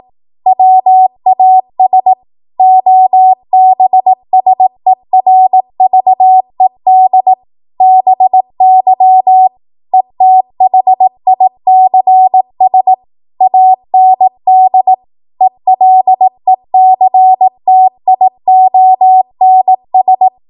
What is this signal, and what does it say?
Morse code, heard as a keyed tone, about 18 wpm, 750 Hz. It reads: WAS OBSERVED BY ETHICS AND ELECTIONS